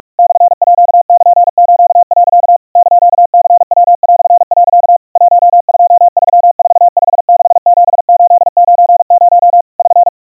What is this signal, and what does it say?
Morse code (Japanese Wabun code), heard as a keyed tone, about 45 wpm, 700 Hz. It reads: モールス、 <KN>ニヤ)、 1234567890 <SN>
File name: iroha 150 cpm 700 Hz Crispy